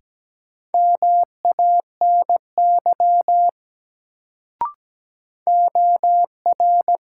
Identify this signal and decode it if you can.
Morse code, heard as a keyed tone, about 17 wpm, 700 Hz. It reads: MANY OR
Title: Sentences from Top 100 Words - New Word Drill - 17wpm